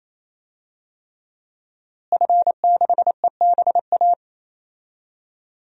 Morse code, heard as a keyed tone, about 28 wpm, 700 Hz. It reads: F6EBA
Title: DXCC-100 Call Signs - 5 Characters - Repeated 3x - 28wpm